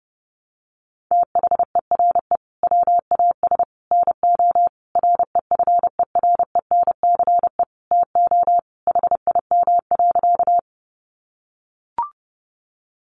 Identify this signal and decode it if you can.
Morse code, heard as a keyed tone, about 30 wpm, 700 Hz. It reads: THERE WAS NO REFERENCE TO HIM.
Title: Sentences from Top 1200 Words 30wpm